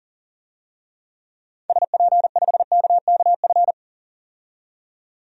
Morse code, heard as a keyed tone, about 40 wpm, 700 Hz. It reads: SP5KKF
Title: DXCC-100 Call Signs - 6 Characters - 40wpm